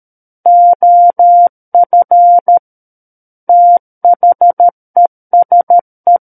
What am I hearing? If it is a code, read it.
Morse code, heard as a keyed tone, about 13 wpm, 700 Hz. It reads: OF THESE
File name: Book One / 04-Chapter 4 - 13 WPM